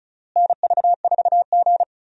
Morse code, heard as a keyed tone, about 34 wpm, 700 Hz. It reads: NV4G
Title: USA Call Signs - 4 Characters - Repeated 3x - 35wpm